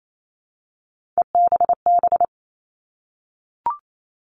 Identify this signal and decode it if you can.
Morse code, heard as a keyed tone, about 28 wpm, 700 Hz. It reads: EBB
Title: All Cut Numbers - 3-Digit 28wpm